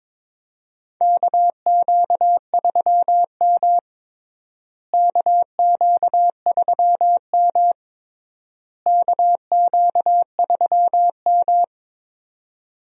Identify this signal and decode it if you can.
Morse code, heard as a keyed tone, about 22 wpm, 700 Hz. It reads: KQ3M KQ3M KQ3M
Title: USA Call Signs - 4 Characters - Repeated 3x - 22wpm